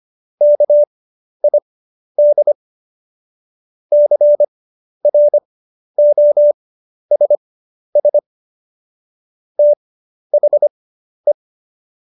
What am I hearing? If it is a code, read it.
Morse code, heard as a keyed tone, about 25 wpm, spaced as for 11 wpm, 590 Hz. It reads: KID CROSS THE
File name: Jokes-25-char-10-eff